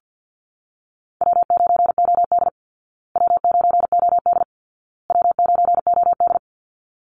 Morse code, heard as a keyed tone, about 48 wpm, 700 Hz. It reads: W9OD W9OD W9OD
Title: Call Signs Repeated 2x - 50wpm